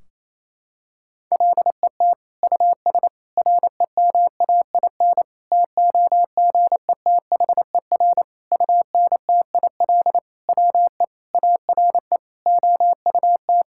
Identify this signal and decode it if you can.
Morse code, heard as a keyed tone, about 27 wpm, 700 Hz. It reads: LET US REMAIN TOGETHER UNTIL WE ARE OUT
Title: Sentences from Top 1600 Words 28wpm